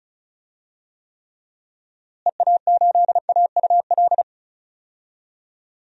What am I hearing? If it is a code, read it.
Morse code, heard as a keyed tone, about 36 wpm, 700 Hz. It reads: EA8AUL
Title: DXCC-100 Call Signs - 6 Characters - Repeated 3x - 35wpm